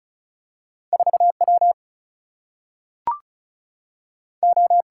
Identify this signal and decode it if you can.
Morse code, heard as a keyed tone, about 36 wpm, 700 Hz. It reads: 4 W O
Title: Single-Letter Single-Number 35wpm